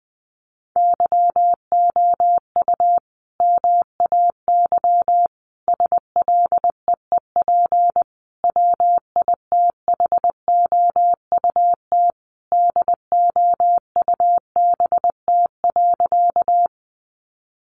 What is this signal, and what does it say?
Morse code, heard as a keyed tone, about 20 wpm, 700 Hz. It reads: YOU MAY SLEEP WITHOUT DOUBT.
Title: Sentences from Top 1500 Words 20wpm